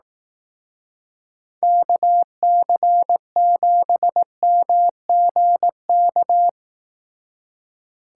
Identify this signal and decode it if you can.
Morse code, heard as a keyed tone, about 18 wpm, 700 Hz. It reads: KC7MGK
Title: Speed Racing - USA 2x3 Call Signs 15wpm